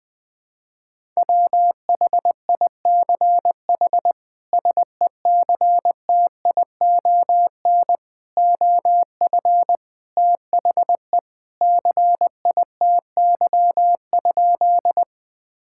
Morse code, heard as a keyed tone, about 19 wpm, 700 Hz. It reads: WHICH SECTION OF THE CITY?
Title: Sentences from Top 1000 Words - Review Part 5 - 20wpm